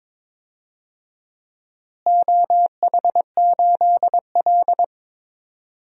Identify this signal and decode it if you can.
Morse code, heard as a keyed tone, about 22 wpm, 700 Hz. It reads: OH8L
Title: DXCC-100 Call Signs - 4 Characters - Repeated 3x - 22wpm